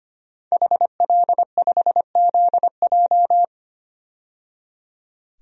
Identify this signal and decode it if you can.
Morse code, heard as a keyed tone, about 25 wpm, 700 Hz. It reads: HL5ZJ
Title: DXCC-100 Call Signs - 5 Characters - 25wpm